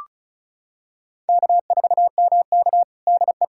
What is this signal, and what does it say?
Morse code, heard as a keyed tone, about 36 wpm, 700 Hz. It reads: K4MK DE
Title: USA Callsign DE Callsign - 4 Characters - 35wpm